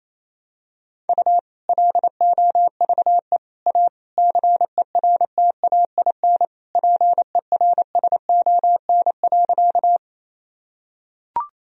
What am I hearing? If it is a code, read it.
Morse code, heard as a keyed tone, about 28 wpm, 700 Hz. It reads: U LOVE A CERTAIN PERSON.
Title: Sentences from Top 500 Words and Abbreviations 28wpm